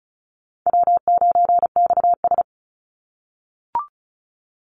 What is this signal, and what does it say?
Morse code, heard as a keyed tone, about 34 wpm, 700 Hz. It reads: W9XS
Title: Call Signs - 35wpm